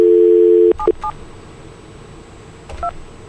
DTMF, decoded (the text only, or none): 77 2